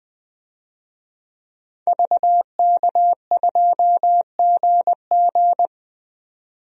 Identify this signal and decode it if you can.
Morse code, heard as a keyed tone, about 20 wpm, 700 Hz. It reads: VK2GG
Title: DXCC-100 Call Signs - 5 Characters - 20wpm